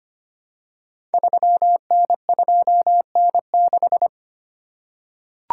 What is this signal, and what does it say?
Morse code, heard as a keyed tone, about 25 wpm, 700 Hz. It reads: 3N2N6
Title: Code Groups - 5 Letter:Numbers - 25wpm